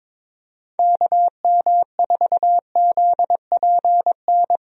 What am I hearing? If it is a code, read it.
Morse code, heard as a keyed tone, about 22 wpm, 700 Hz. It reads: KM4ZPN
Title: USA Call Signs - 6 Characters - Repeated 2x - 22wpm